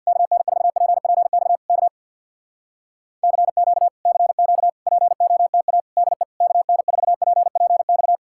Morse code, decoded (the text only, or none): KN4PPX K CQ CQ POTA DE KN4PPX